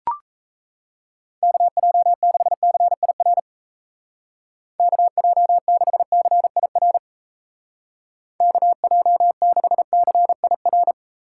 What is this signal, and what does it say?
Morse code, heard as a keyed tone, about 35 wpm, 700 Hz. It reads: KJ6CIR KJ6CIR KJ6CIR
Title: Speed Racing - USA 2x3 Call Signs 28wpm